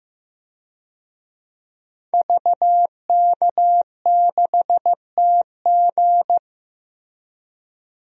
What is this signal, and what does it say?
Morse code, heard as a keyed tone, about 16 wpm, 700 Hz. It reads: VK6TG
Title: DXCC-100 Call Signs - 5 Characters - Repeated 3x - 15wpm